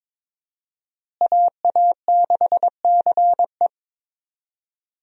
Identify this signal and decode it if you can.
Morse code, heard as a keyed tone, about 23 wpm, 700 Hz. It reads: AA6CE
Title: USA Call Signs - 5 Characters - Repeated 3x - 22wpm